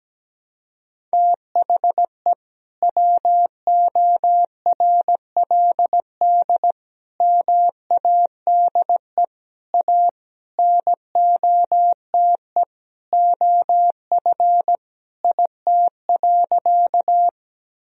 Morse code, heard as a keyed tone, about 17 wpm, 700 Hz. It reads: THE WORLD MADE A NOTE OF IT.
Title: Sentences from Top 1000 Words - Review Part 2 - 17wpm